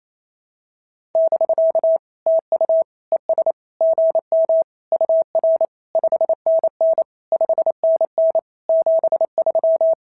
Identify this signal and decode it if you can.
Morse code, heard as a keyed tone, about 29 wpm, 690 Hz. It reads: <BK> TU ES GM UR 5NN 5NN 73
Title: SOTA QSO Practice 28wpm